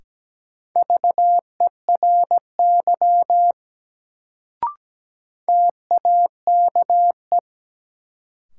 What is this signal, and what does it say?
Morse code, heard as a keyed tone, about 17 wpm, 700 Hz. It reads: VERY TAKE
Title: Sentences from Top 200 Words - New Word Drill - 17wpm